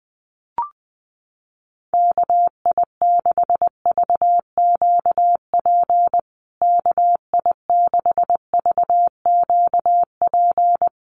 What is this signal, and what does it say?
Morse code, heard as a keyed tone, about 20 wpm, 700 Hz. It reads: KI6VQP KI6VQP
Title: USA Call Signs Repeated 3x 20wpm